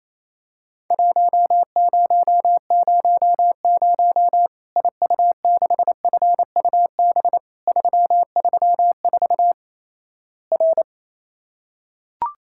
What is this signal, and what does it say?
Morse code, heard as a keyed tone, about 28 wpm, 700 Hz. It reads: 1000 IU6FUB 334 R
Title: Worked All Europe DX Contest - QTC Exchange - 28wpm